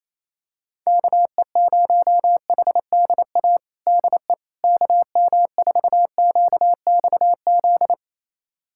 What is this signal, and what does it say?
Morse code, heard as a keyed tone, about 27 wpm, 700 Hz. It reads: KE0HDA DE KM4QXZ
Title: USA Callsign DE Callsign - 6 Characters - 28wpm